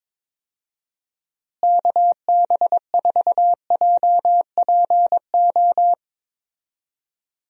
Morse code, heard as a keyed tone, about 22 wpm, 700 Hz. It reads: KB4JPO
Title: USA Call Signs - 6 Characters - Repeated 3x - 22wpm